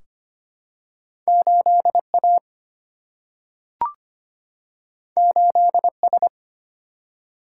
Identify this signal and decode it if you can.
Morse code, heard as a keyed tone, about 25 wpm, 700 Hz. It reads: 8 A 8 S
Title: Single-Letter Single-Number 25wpm